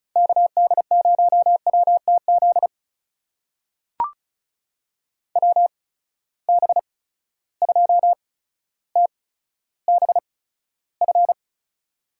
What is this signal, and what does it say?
Morse code, heard as a keyed tone, about 36 wpm, spaced as for 15 wpm, 700 Hz. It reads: KD0WTZ WB2TBF
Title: USA Call Signs - 6 Characters - 8x Character Spacing - 35wpm